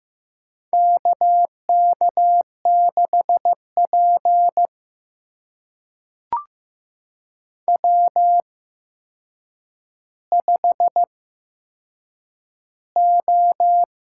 Morse code, heard as a keyed tone, about 15 wpm, spaced as for 7 wpm, 700 Hz. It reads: KK6P W5O
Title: USA Call Signs - 4 Characters - 8x Character Spacing - 15wpm